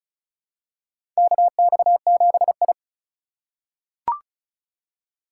Morse code, heard as a keyed tone, about 34 wpm, 700 Hz. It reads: KX7I
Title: USA Call Signs - 4 Characters - 3x Character Spacing - 35wpm